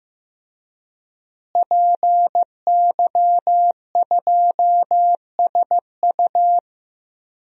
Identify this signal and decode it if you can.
Morse code, heard as a keyed tone, about 15 wpm, 700 Hz. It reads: PY2SU
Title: DXCC-100 Call Signs - 5 Characters - Repeated 2x - 15wpm